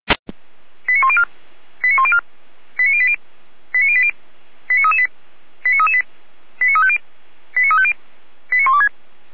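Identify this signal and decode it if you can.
Five-tone selective call (ZVEI), heard as quick runs of five tones, tones of about 70 ms. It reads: 89104 89104 89080 89080 89338 89338 89350 89350 89126
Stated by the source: ZVEI five-tone selective call received by SDR.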